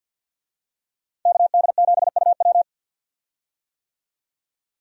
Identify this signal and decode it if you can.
Morse code, heard as a keyed tone, about 51 wpm, 700 Hz. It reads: KD7UW
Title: USA Call Signs - 5 Characters - 50wpm